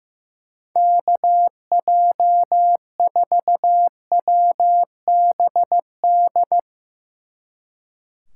Morse code, heard as a keyed tone, about 16 wpm, 700 Hz. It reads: KJ4WBD
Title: Speed Racing - USA 2x3 Call Signs 15wpm